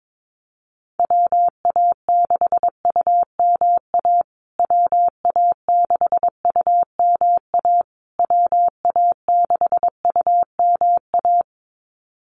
Morse code, heard as a keyed tone, about 22 wpm, 700 Hz. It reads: WA6UMA WA6UMA WA6UMA
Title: USA Call Signs Repeated 3x 22wpm